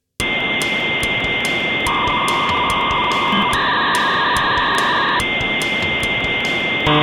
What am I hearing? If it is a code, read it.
Dial-up modem handshake.